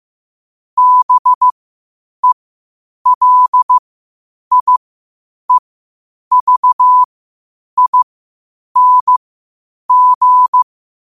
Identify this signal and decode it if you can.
Morse code, heard as a keyed tone, about 15 wpm, spaced as for 8 wpm, 1000 Hz. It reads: BELIEVING
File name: qotdfiles / 8wpm